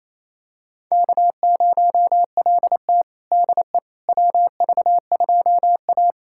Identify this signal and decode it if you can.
Morse code, heard as a keyed tone, about 27 wpm, 690 Hz. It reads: K0LT DE WV2A
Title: USA Callsign DE Callsign - 4 Characters - 28wpm